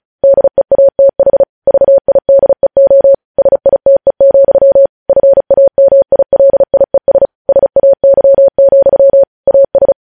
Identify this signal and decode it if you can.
Morse code, heard as a keyed tone, about 36 wpm, 550 Hz. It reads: DEATH VIDEO SITE, FAMILIES SAY, AS
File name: podcast / News headlines-1x-35x35